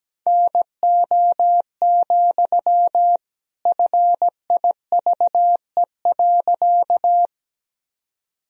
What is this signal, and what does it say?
Morse code, heard as a keyed tone, about 17 wpm, 700 Hz. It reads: NO, FIVE.
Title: Sentences from Top 1000 Words - Review Part 2 - 17wpm